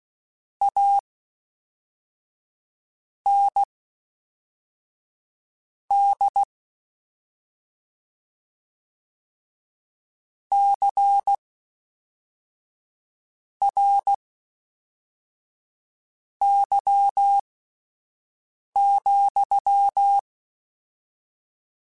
Morse code, heard as a keyed tone, about 16 wpm, spaced as for 4 wpm, 800 Hz. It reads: AND CRY,
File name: Misc / Misc-mp3s / WordPairs sp @5